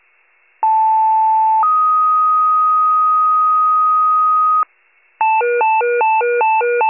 Two-tone page, heard as a one-second tone, then a three-second tone, about 880 then 1250 Hz. It is heard from 0.6 to 4.6 s.
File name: wcalert